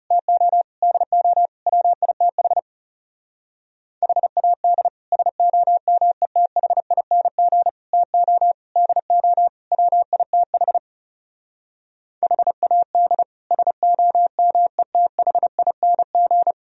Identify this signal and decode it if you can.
Morse code, heard as a keyed tone, about 40 wpm, 700 Hz. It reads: TO DO WITH HAD SOMETHING TO DO WITH HAD SOMETHING